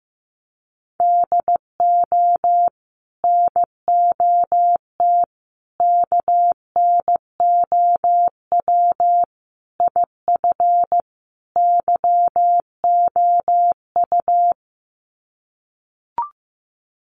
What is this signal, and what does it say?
Morse code, heard as a keyed tone, about 15 wpm, 700 Hz. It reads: DO NOT KNOW IF YOU
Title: Sets of 5 Words 15wpm